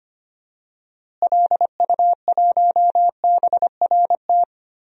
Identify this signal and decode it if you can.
Morse code, heard as a keyed tone, about 25 wpm, 700 Hz. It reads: LU1BRT
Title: DXCC-100 Call Signs - 6 Characters - Repeated 3x - 25wpm